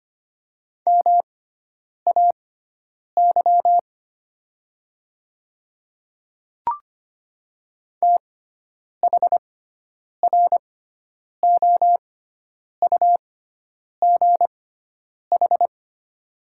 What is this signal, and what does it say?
Morse code, heard as a keyed tone, about 25 wpm, spaced as for 9 wpm, 700 Hz. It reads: MAY THROUGH
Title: 6x Character Spacing - Top 200 Words - 25wpm